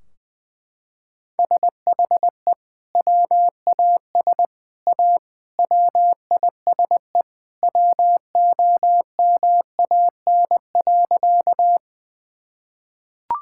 Morse code, heard as a keyed tone, about 20 wpm, 700 Hz. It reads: SHE WAS A WISE WOMAN.
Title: Sentences from Top 1600 Words 20wpm